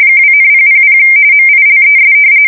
RTTY, about 45.45 baud. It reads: 2,000 MILES
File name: fsk170 / battleshipnj